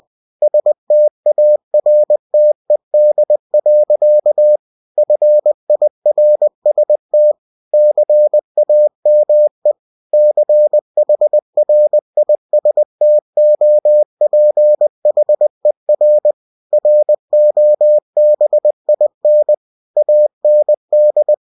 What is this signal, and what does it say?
Morse code, heard as a keyed tone, about 20 wpm, 600 Hz. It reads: STARTED. FIRST CAME CHRISTOPHER ROBIN AND
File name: cw-books / winnie-the-pooh / winnie-20wpm-page-0079